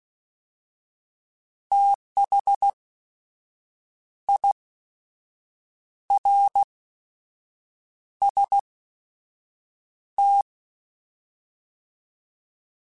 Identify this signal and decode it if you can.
Morse code, heard as a keyed tone, about 16 wpm, spaced as for 6 wpm, 800 Hz. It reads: THIRST
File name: TH / TH mp3s / Fast TH @15